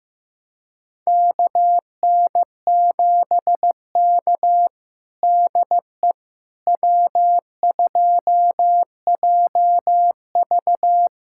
Morse code, heard as a keyed tone, about 15 wpm, 700 Hz. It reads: KN7K DE W2JV
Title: USA Callsign DE Callsign - 4 Characters - 15wpm